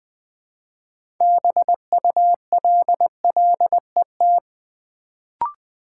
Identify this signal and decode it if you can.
Morse code, heard as a keyed tone, about 20 wpm, 700 Hz. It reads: BULLET
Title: Build A Word - 2 Letters At A Time - 6-Letter Words - 20wpm